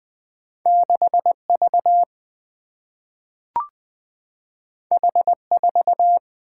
Morse code, heard as a keyed tone, about 20 wpm, 700 Hz. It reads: 6V H4
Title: Single-Letter Single-Number 20wpm